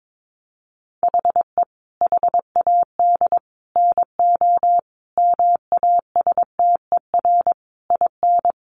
Morse code, heard as a keyed tone, about 22 wpm, 700 Hz. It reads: HE HAD NO MASTER IN
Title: Sentences from Top 1300 Words 22wpm